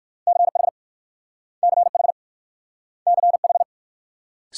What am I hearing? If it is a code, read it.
Morse code, heard as a keyed tone, about 55 wpm, 700 Hz. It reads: CH CH CH
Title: Speed-Racing Multi-letter Phonemes - 45wpm